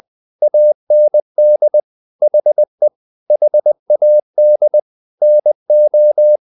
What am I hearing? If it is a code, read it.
Morse code, heard as a keyed tone, about 20 wpm, 600 Hz. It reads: AND HE HAD NO
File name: the old man and the sea part 2 20wpm